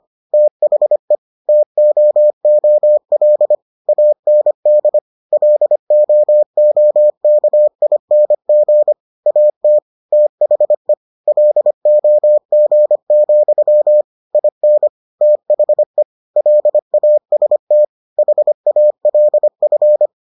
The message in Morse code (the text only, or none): THE TOOL AND LOOKING AT THE LOG, IN THE LAST HALF